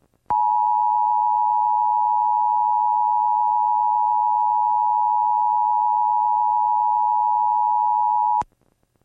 Emergency Alert System (SAME) tones (the 853 and 960 Hz attention tone). It is heard from 0.2 to 8.4 s.
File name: EAS on KDVS 11-6-06